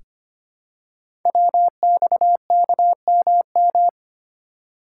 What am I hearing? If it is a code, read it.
Morse code, heard as a keyed tone, about 25 wpm, 700 Hz. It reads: WXKMM
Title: Code Groups - 5 Letters - 25wpm